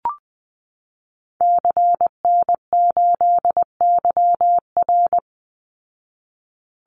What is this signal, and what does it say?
Morse code, heard as a keyed tone, about 20 wpm, 700 Hz. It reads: CN8YR
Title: Call Signs Repeated 2x - 20wpm